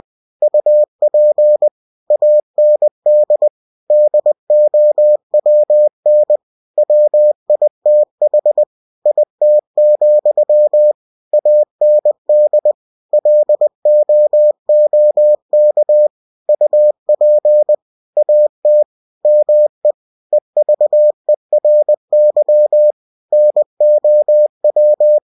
Morse code, heard as a keyed tone, about 20 wpm, 600 Hz. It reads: UP AND DOWN WITH IT, AND LOOK UP AT ME EVERY NOW